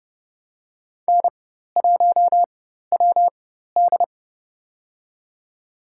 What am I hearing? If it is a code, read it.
Morse code, heard as a keyed tone, about 30 wpm, spaced as for 14 wpm, 700 Hz. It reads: N1WD
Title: USA Call Signs - 4 Characters - 4x Character Spacing - 30wpm